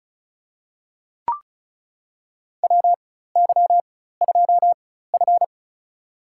Morse code, heard as a keyed tone, about 36 wpm, spaced as for 16 wpm, 700 Hz. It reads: WY2F
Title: USA Call Signs - 4x Character Spacing - 35wpm